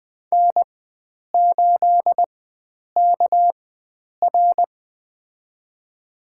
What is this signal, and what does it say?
Morse code, heard as a keyed tone, about 21 wpm, spaced as for 9 wpm, 700 Hz. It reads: N8KR
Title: USA Call Signs - 4 Characters - 4x Character Spacing - 20wpm